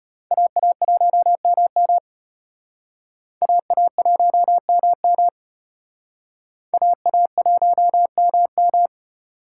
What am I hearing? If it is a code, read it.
Morse code, heard as a keyed tone, about 33 wpm, 700 Hz. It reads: AA1MM AA1MM AA1MM